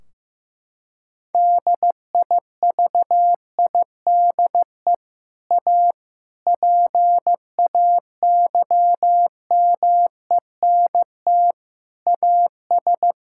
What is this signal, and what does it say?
Morse code, heard as a keyed tone, about 15 wpm, 700 Hz. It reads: DIVIDE A PAYMENT AS
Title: Sentences from Top 1600 Words 15wpm